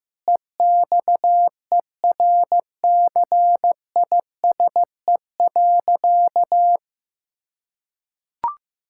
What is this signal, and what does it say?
Morse code, heard as a keyed tone, about 15 wpm, 700 Hz. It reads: EXERCISE.
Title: Sentences from Top 1000 Words - Review Part 6 - 15wpm